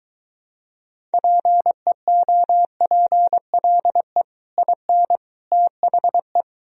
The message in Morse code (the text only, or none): PEOPLE IN THE